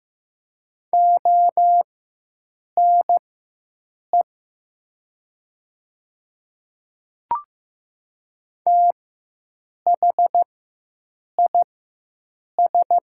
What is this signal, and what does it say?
Morse code, heard as a keyed tone, about 15 wpm, spaced as for 7 wpm, 700 Hz. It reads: ONE THIS
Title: 4x Character Spacing - Top 300 Words - 15wpm